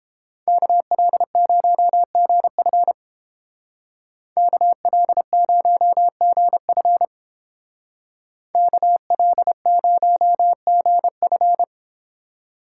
Morse code, heard as a keyed tone, about 29 wpm, 700 Hz. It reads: KL0GF KL0GF KL0GF